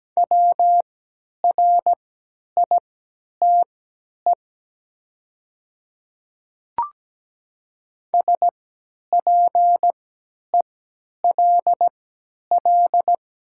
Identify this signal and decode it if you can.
Morse code, heard as a keyed tone, about 17 wpm, spaced as for 10 wpm, 700 Hz. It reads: WRITE SPELL
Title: 3x Character Spacing - Top 200 Words - 17wpm